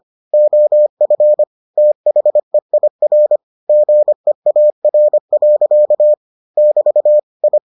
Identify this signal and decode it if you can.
Morse code, heard as a keyed tone, about 25 wpm, 600 Hz. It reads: OF THEIR GEAR. <BT> I